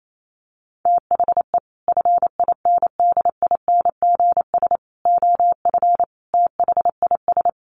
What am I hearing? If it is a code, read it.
Morse code, heard as a keyed tone, about 28 wpm, 700 Hz. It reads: THE FINDINGS OF THIS
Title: Sets of 5 Words Encore2 28wpm